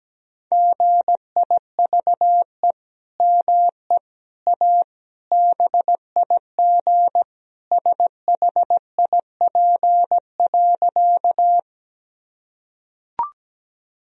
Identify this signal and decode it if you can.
Morse code, heard as a keyed tone, about 17 wpm, 700 Hz. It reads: GIVE ME A BIG SHIP.
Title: Sentences from Top 1000 Words - Review Part 2 - 17wpm